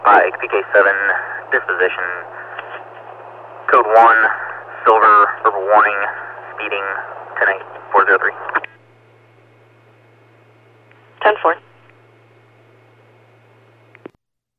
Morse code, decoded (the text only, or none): none